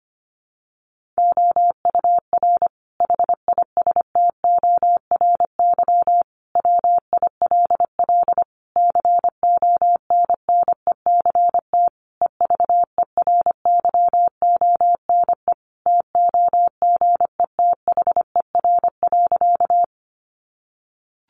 Morse code, heard as a keyed tone, about 25 wpm, 700 Hz. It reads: OUR HISTORY WILL CONNECT EVERYONE TOGETHER.
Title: Sentences from Top 1500 Words 25wpm